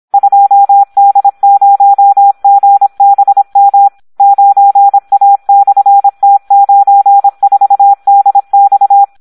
Morse code, heard as a keyed tone, about 26 wpm, 800 Hz. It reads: 2D0GBM 9A/T94DX
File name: Audio practica CW / SD 1 26